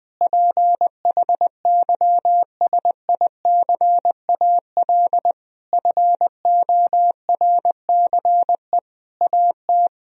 Morse code, heard as a keyed tone, about 21 wpm, 700 Hz. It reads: PHYSICAL FORCE AT
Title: Sentences from Top 1000 Words - Review Part 4 - 20wpm